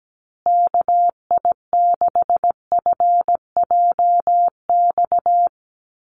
Morse code, heard as a keyed tone, about 17 wpm, 700 Hz. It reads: KI6FJX
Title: USA Call Signs Repeated 3x 17wpm